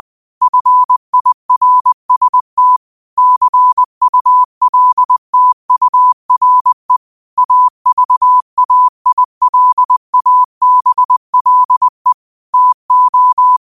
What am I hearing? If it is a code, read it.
Morse code, heard as a keyed tone, about 20 wpm, 1000 Hz. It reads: FIRST CULTURE AVAILABLE TO